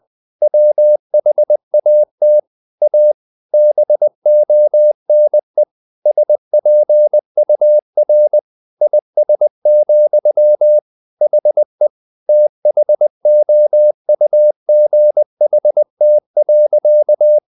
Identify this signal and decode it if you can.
Morse code, heard as a keyed tone, about 20 wpm, 600 Hz. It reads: WHAT A BONE SPUR IS, HE THOUGHT.
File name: the old man and the sea part 2 20wpm